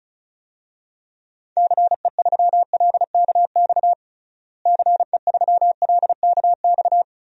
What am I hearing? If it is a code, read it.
Morse code, heard as a keyed tone, about 34 wpm, 700 Hz. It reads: CE3LKX CE3LKX
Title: DXCC-100 Call Signs - 6 Characters - Repeated 2x - 35wpm